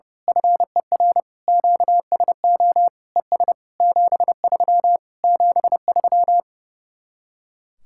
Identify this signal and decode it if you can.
Morse code, heard as a keyed tone, about 28 wpm, 700 Hz. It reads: FER QSO ES 73 73
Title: Speed-Racing Common QSO Phrases - 30wpm